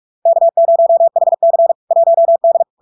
Morse code, heard as a keyed tone, about 46 wpm, 670 Hz. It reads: K0HC 1D
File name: LARK-FD-2022-Log-in-CW-45wpm-670-Hz